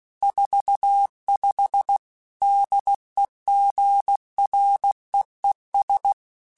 Morse code, heard as a keyed tone, about 16 wpm, 800 Hz. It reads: 45 DEGREES
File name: Weather Sets / Weather Sets-mp3s / Weather 3 17 @15